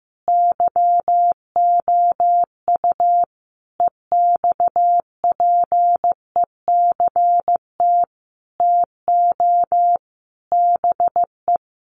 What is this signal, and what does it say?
Morse code, heard as a keyed tone, about 15 wpm, 700 Hz. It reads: YOU EXPECT TO BE
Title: Sentences from Top 1100 Words 15wpm